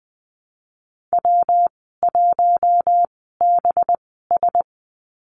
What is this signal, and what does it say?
Morse code, heard as a keyed tone, about 20 wpm, spaced as for 14 wpm, 700 Hz. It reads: W1BS
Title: 186 Lesson 30g - Letter b - Call Signs 20wpm